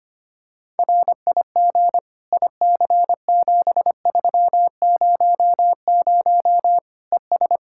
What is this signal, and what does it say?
Morse code, heard as a keyed tone, about 25 wpm, 700 Hz. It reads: RIG IC7300 ES
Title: Practice QSO - Exchange 2 - Info - 25wpm